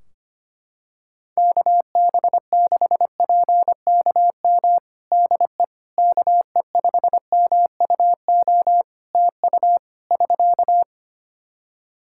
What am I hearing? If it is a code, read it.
Morse code, heard as a keyed tone, about 25 wpm, 700 Hz. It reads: KB6PKM DE KE5MUO TU <SK>
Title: Practice QSO - Exchange 4 - Ending - 25wpm